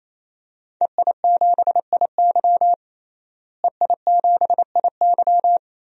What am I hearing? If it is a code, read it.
Morse code, heard as a keyed tone, about 28 wpm, 700 Hz. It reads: EI7IY EI7IY
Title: DXCC-100 Call Signs - 5 Characters - Repeated 3x - 28wpm